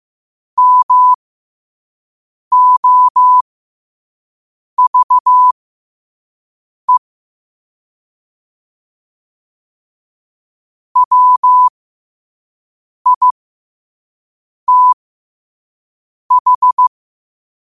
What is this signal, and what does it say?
Morse code, heard as a keyed tone, about 15 wpm, spaced as for 5 wpm, 1000 Hz. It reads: MOVE WITH
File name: qotdfiles / 5wpm